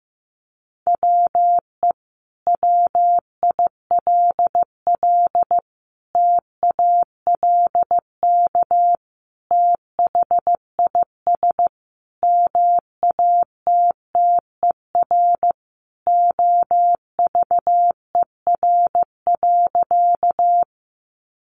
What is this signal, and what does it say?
Morse code, heard as a keyed tone, about 15 wpm, 700 Hz. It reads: WE WILL TALK THIS MATTER OVER.
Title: Sentences from Top 700 Words 15wpm